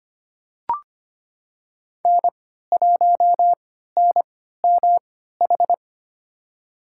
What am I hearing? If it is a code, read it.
Morse code, heard as a keyed tone, about 25 wpm, spaced as for 14 wpm, 700 Hz. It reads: N1NMH
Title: USA Call Signs - 5 Characters - 3x Character Spacing - 25wpm